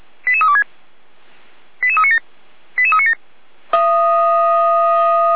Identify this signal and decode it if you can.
Five-tone selective call (ZVEI), heard as quick runs of five tones, tones of about 80 ms. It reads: 88326 88397 88397